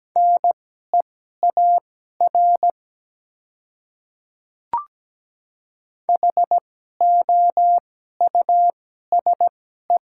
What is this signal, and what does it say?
Morse code, heard as a keyed tone, about 17 wpm, spaced as for 12 wpm, 700 Hz. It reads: NEAR HOUSE
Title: 2x Character Spacing - Top 200 Words - 17wpm